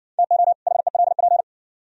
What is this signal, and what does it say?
Morse code, heard as a keyed tone, about 56 wpm, 700 Hz. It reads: TO HELP